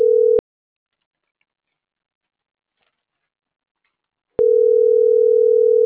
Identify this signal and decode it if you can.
Ringback tone (North America), 440 + 480 Hz.